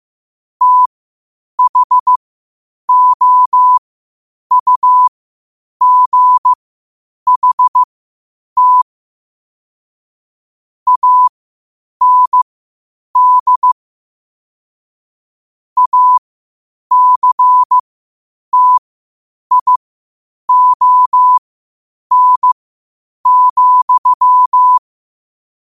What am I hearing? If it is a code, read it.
Morse code, heard as a keyed tone, about 15 wpm, spaced as for 8 wpm, 1000 Hz. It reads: THOUGHT AND ACTION,
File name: qotdfiles / 8wpm